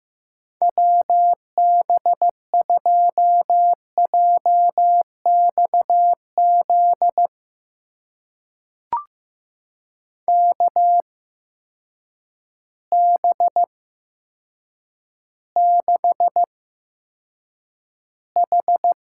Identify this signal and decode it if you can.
Morse code, heard as a keyed tone, about 15 wpm, spaced as for 8 wpm, 700 Hz. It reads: WB2JXZ KB6H
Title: USA Call Signs - 6 Characters - 8x Character Spacing - 15wpm